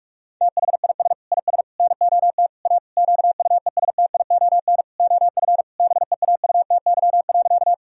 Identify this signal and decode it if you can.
Morse code, heard as a keyed tone, about 46 wpm, 700 Hz. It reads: THIS IS NOT A QUESTION OF BEAUTY.
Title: Sentences from Top 1000 Words - Review Part 2 - 45wpm